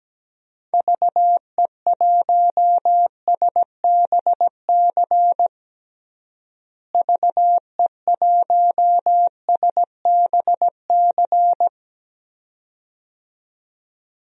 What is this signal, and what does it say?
Morse code, heard as a keyed tone, about 17 wpm, 700 Hz. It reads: VE1SBC VE1SBC
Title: DXCC-100 Call Signs - 6 Characters - Repeated 2x - 17wpm